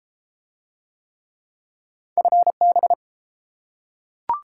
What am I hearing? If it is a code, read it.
Morse code, heard as a keyed tone, about 30 wpm, 700 Hz. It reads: FB